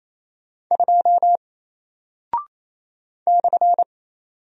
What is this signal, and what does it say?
Morse code, heard as a keyed tone, about 28 wpm, 700 Hz. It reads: 2 /
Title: Single Character (Letter:Number:Punctuation:Prosign) Alt 28wpm